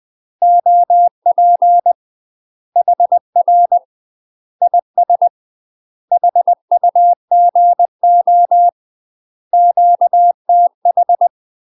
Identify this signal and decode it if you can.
Morse code, heard as a keyed tone, about 20 wpm, 700 Hz. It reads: OP HR IS HUGO QTH
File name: UK MASTER QSOs TRAINING / 20WPM / UK MASTER QSOs 0015